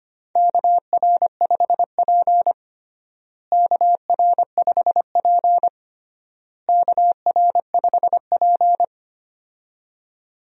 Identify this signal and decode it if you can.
Morse code, heard as a keyed tone, about 25 wpm, 700 Hz. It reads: KR5P KR5P KR5P
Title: USA Call Signs - 4 Characters - Repeated 3x - 25wpm